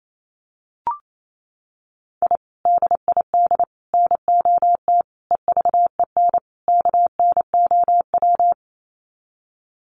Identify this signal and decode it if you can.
Morse code, heard as a keyed tone, about 28 wpm, 700 Hz. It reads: I DID NOT EVEN KNOW
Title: Sets of 5 Words Encore 28wpm